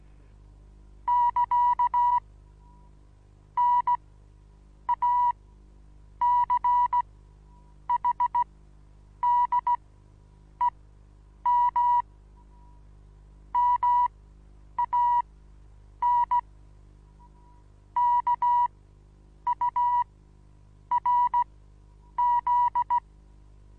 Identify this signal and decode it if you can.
Morse code, heard as a keyed tone, about 16 wpm, spaced as for 8 wpm, 970 Hz. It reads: <KA> NACHDEM MAN KURZ
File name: Onlinelehrgang / morsen / DJ4UF Morsen25